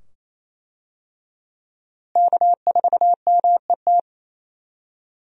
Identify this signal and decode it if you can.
Morse code, heard as a keyed tone, about 28 wpm, 700 Hz. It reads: K4MET
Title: USA Call Signs - 5 Characters - Repeated 2x - 28wpm